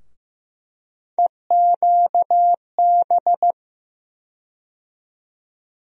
Morse code, heard as a keyed tone, about 15 wpm, 700 Hz. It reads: EQB
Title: USA Call Sign Postfixes - 3 Letters 15wpm